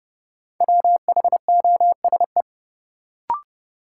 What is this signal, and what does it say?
Morse code, heard as a keyed tone, about 30 wpm, 700 Hz. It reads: WHOSE
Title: Sentences from Top 1200 Words - New Word Drill - 30wpm